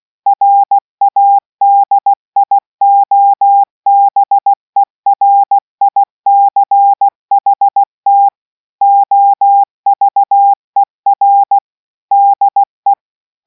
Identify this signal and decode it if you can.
Morse code, heard as a keyed tone, about 16 wpm, 800 Hz. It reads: RADIOBERICHT OVER DE